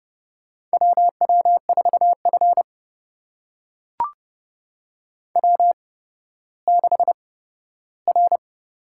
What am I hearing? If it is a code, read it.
Morse code, heard as a keyed tone, about 29 wpm, spaced as for 14 wpm, 700 Hz. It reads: WW4F W6R
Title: USA Call Signs - 4 Characters - 8x Character Spacing - 30wpm